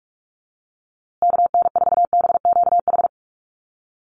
Morse code, heard as a keyed tone, about 44 wpm, 700 Hz. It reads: KN4BQH
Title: USA Call Signs Repeated 2x 45wpm